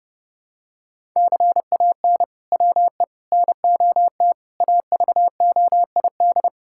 Morse code, heard as a keyed tone, about 30 wpm, 700 Hz. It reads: CAN WE NOT AVOID
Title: Sentences from Top 1000 Words - Review Part 6 - 30wpm